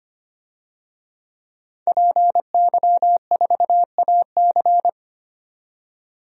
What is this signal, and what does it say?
Morse code, heard as a keyed tone, about 25 wpm, 700 Hz. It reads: PY4AC
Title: DXCC-100 Call Signs - 5 Characters - 25wpm